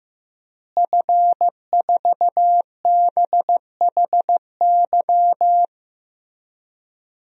Morse code, heard as a keyed tone, about 15 wpm, 700 Hz. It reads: F4BHY
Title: DXCC-100 Call Signs - 5 Characters - Repeated 3x - 15wpm